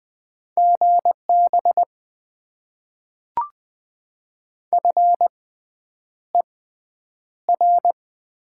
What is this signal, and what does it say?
Morse code, heard as a keyed tone, about 20 wpm, spaced as for 9 wpm, 700 Hz. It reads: GB FER
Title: Top 90 QSO Elements - 6x Character Spacing - 20wpm